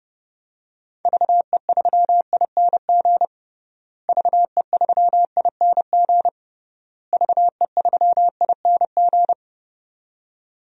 Morse code, heard as a keyed tone, about 31 wpm, 700 Hz. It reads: VE3ING VE3ING VE3ING
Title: DXCC-100 Call Signs - 6 Characters - Repeated 3x - 30wpm